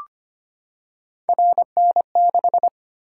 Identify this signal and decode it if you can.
Morse code, heard as a keyed tone, about 25 wpm, 700 Hz. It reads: RN6
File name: Three Letter-Number Combinations - 25wpm